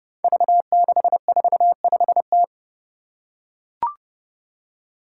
Morse code, heard as a keyed tone, about 29 wpm, 700 Hz. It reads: V645T
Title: Five Challenging Letter-Number Combinations 30wpm